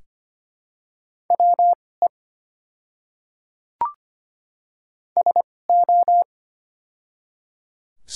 Morse code, heard as a keyed tone, about 25 wpm, spaced as for 18 wpm, 700 Hz. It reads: WE SO
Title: 2x Character Spacing - Top 200 Words - 25wpm